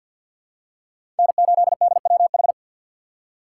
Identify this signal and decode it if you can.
Morse code, heard as a keyed tone, about 51 wpm, 700 Hz. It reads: N8DWH
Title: USA Call Signs - 4x Character Spacing - 50wpm